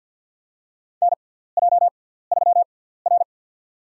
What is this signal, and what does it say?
Morse code, heard as a keyed tone, about 48 wpm, spaced as for 17 wpm, 700 Hz. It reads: NJ3R
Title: USA Call Signs - 4 Characters - 6x Character Spacing - 50wpm